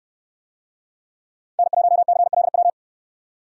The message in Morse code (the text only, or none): N1CLW